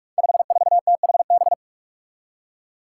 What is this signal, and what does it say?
Morse code, heard as a keyed tone, about 46 wpm, 700 Hz. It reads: 54THB
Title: Five Challenging Letter-Number Combinations 45wpm